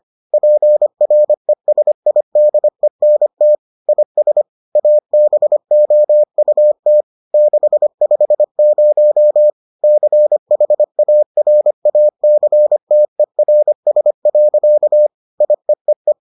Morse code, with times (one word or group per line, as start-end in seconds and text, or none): 0.3-3.5 PRESIDENT
3.9-4.4 IS
4.8-7.0 ABOUT
7.3-9.5 650
9.8-15.1 CHARACTERS.
15.4-16.1 IEEE